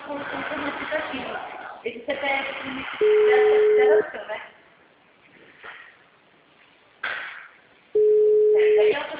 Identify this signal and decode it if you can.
Ringback tone (Europe, 425 Hz), with speech.